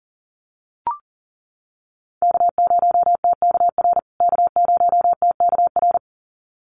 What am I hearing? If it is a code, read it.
Morse code, heard as a keyed tone, about 38 wpm, 700 Hz. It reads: K0TKR K0TKR
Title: USA Call Signs Repeated 2x 40wpm